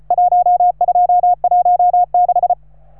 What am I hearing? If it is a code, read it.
Morse code, heard as a keyed tone, about 36 wpm, 720 Hz. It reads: 1216